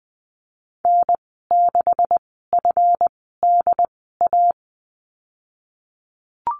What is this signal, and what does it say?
Morse code, heard as a keyed tone, about 20 wpm, spaced as for 15 wpm, 700 Hz. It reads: N6FDA
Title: 167 Lesson 27g - Number 7 - Call Signs 20wpm